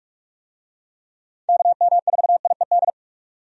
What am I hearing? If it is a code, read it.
Morse code, heard as a keyed tone, about 44 wpm, 700 Hz. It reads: KM4IED
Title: USA Call Signs - 6 Characters - 45wpm